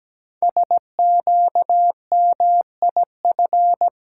Morse code, heard as a keyed tone, about 17 wpm, 700 Hz. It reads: SQMIF
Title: Code Groups - 5 Letters - 17wpm